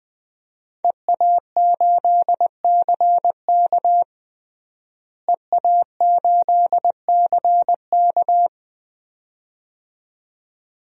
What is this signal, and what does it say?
Morse code, heard as a keyed tone, about 20 wpm, 700 Hz. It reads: EA8CK EA8CK
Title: DXCC-100 Call Signs - 5 Characters - Repeated 2x - 20wpm